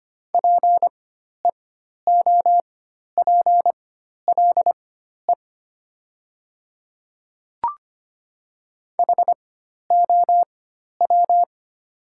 Morse code, heard as a keyed tone, about 25 wpm, spaced as for 12 wpm, 700 Hz. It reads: PEOPLE HOW
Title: 4x Character Spacing - Top 100 Words - 25wpm